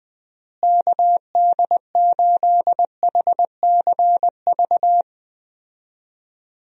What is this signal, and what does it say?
Morse code, heard as a keyed tone, about 20 wpm, 700 Hz. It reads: KD8HCV
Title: USA Call Signs - 6 Characters - Repeated 2x - 20wpm